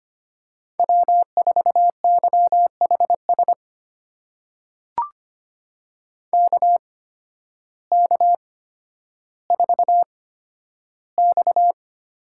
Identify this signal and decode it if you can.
Morse code, heard as a keyed tone, about 25 wpm, spaced as for 12 wpm, 700 Hz. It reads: W4YHS KK4X
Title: USA Call Signs - 8x Character Spacing - 25wpm